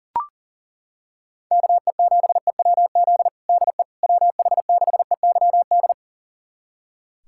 Morse code, heard as a keyed tone, about 40 wpm, 700 Hz. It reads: KE7EWZ DE WH6EYD
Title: USA Callsign DE Callsign 40wpm